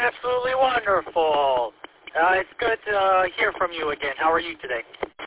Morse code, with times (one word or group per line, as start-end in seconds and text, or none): none